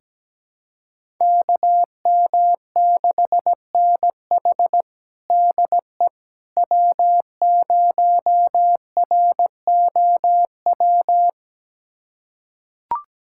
Morse code, heard as a keyed tone, about 17 wpm, 700 Hz. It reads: KM6NH DE W0ROW
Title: USA Callsign DE Callsign - 5 Characters - 17wpm